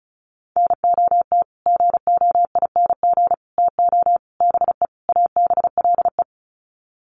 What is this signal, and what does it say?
Morse code, heard as a keyed tone, about 34 wpm, 700 Hz. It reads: NOT GOING TO BE ABLE
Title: Sets of 5 Words 35wpm